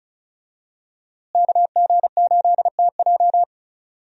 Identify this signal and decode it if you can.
Morse code, heard as a keyed tone, about 36 wpm, 700 Hz. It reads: KG8TJ
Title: USA Call Signs - 5 Characters - Repeated 2x - 35wpm